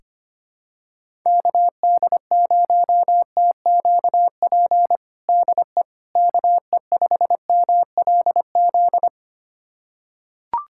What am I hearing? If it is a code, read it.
Morse code, heard as a keyed tone, about 25 wpm, 700 Hz. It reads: KD0TQP DE KE5MLZ
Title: USA Callsign DE Callsign - 6 Characters - 25wpm